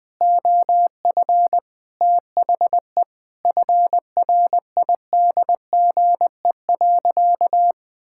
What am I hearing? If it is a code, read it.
Morse code, heard as a keyed tone, about 19 wpm, 700 Hz. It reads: OF THE FRIDGE.
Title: Sentences with 4-Word Phrases - Spoken Beginning - 20wpm